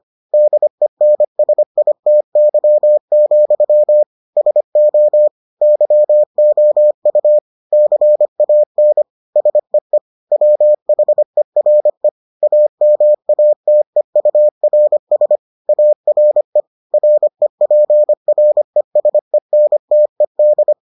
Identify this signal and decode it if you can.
Morse code, heard as a keyed tone, about 25 wpm, 600 Hz. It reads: DENSITY, SO YOU CAN SEE WHERE AMATEURS ARE REPRESENTED